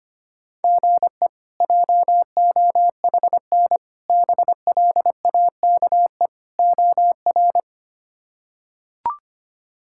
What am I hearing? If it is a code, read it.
Morse code, heard as a keyed tone, about 25 wpm, 700 Hz. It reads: GE JOHN BLAKE OR
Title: K1USN SST QSO Practice - 25wpm